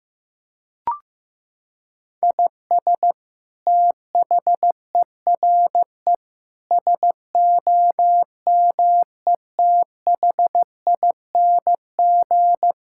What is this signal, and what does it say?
Morse code, heard as a keyed tone, about 16 wpm, 700 Hz. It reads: IS THERE SOMETHING
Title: Sentences from Top 1000 Words - Review Part 6 - 15wpm